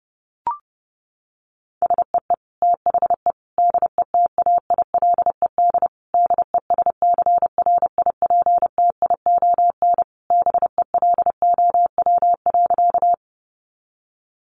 Morse code, heard as a keyed tone, about 30 wpm, 700 Hz. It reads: SEE THE DETAILED DESCRIPTION BELOW.
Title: Sentences from Top 1500 Words 30wpm